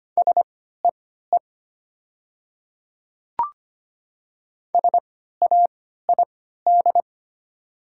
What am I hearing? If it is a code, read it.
Morse code, heard as a keyed tone, about 25 wpm, spaced as for 14 wpm, 700 Hz. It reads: SEE SAID
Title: 3x Character Spacing - Top 100 Words - 25wpm